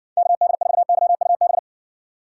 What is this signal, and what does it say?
Morse code, heard as a keyed tone, about 60 wpm, 700 Hz. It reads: KD4YUB